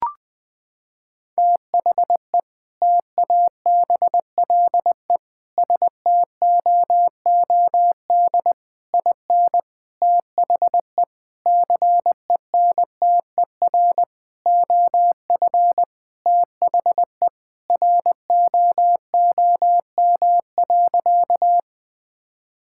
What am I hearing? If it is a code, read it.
Morse code, heard as a keyed tone, about 20 wpm, 700 Hz. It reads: THE TABLE STOOD IN THE CENTER OF THE ROOM.
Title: Sentences with 4-Word Phrases - 20wpm